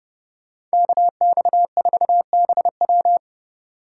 Morse code, heard as a keyed tone, about 37 wpm, 700 Hz. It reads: KX4BW
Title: Speed Racing - USA Call Signs 30wpm